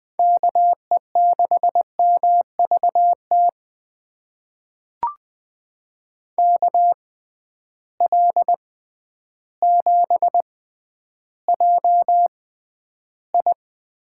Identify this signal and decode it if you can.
Morse code, heard as a keyed tone, about 20 wpm, spaced as for 11 wpm, 700 Hz. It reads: KE6MVT KL7JI
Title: USA Call Signs - 6 Characters - 6x Character Spacing - 20wpm